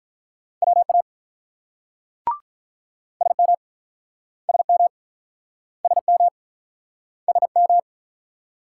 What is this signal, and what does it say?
Morse code, heard as a keyed tone, about 48 wpm, 700 Hz. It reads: WA SM SM SM SM